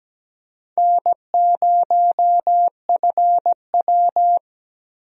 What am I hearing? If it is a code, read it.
Morse code, heard as a keyed tone, about 17 wpm, 700 Hz. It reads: N0FW
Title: USA Call Signs - 4 Characters - Repeated 2x - 17wpm